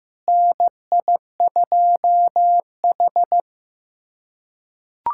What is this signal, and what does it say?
Morse code, heard as a keyed tone, about 15 wpm, 700 Hz. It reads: NI2H
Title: USA Callsign DE Callsign - 4 Characters - 15wpm